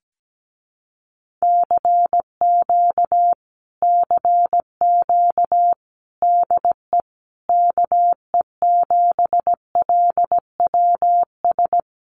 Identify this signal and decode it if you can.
Morse code, heard as a keyed tone, about 17 wpm, 700 Hz. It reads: CQ CQ DE KE7LWS
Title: Common QSO Phrases Encore 17wpm